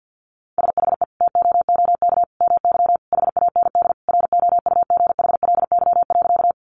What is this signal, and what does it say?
Morse code, heard as a keyed tone, about 50 wpm, 700 Hz. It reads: SHE TOOK MY HAND ROUGHLY.
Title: Sentences from Top 1500 Words 50wpm